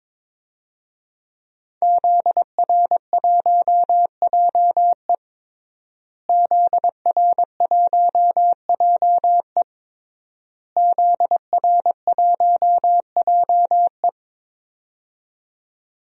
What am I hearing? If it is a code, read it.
Morse code, heard as a keyed tone, about 22 wpm, 700 Hz. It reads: ZR1JE ZR1JE ZR1JE
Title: DXCC-100 Call Signs - 5 Characters - Repeated 3x - 22wpm